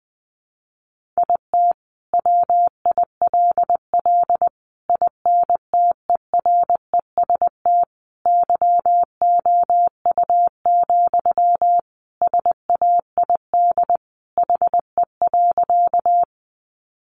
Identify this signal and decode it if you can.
Morse code, heard as a keyed tone, about 20 wpm, 700 Hz. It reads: IT WILL INTEREST YOU, SAID HE.
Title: Sentences from Top 400 Words 20wpm